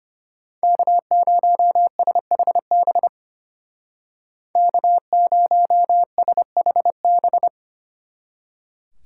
Morse code, heard as a keyed tone, about 30 wpm, 700 Hz. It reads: K0SHB K0SHB
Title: Speed Racing - USA 1x3 Call Signs 25wpm